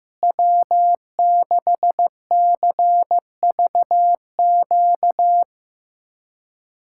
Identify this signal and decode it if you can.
Morse code, heard as a keyed tone, about 16 wpm, 700 Hz. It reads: W6CVQ
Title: USA Callsign DE Callsign 15wpm